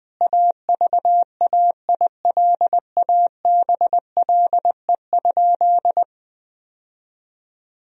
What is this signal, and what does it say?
Morse code, heard as a keyed tone, about 20 wpm, 700 Hz. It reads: AVAILABLE?
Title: Sentences from Top 1000 Words - Review Part 3 - 20wpm